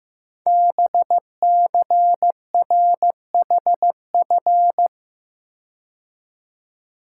Morse code, heard as a keyed tone, about 15 wpm, 700 Hz. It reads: BCRHF
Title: Code Groups - 5 Letters - 15wpm